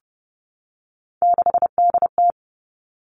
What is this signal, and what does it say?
Morse code, heard as a keyed tone, about 30 wpm, 700 Hz. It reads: 6DT
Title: Three Challenging Letter-Number Combinations 30wpm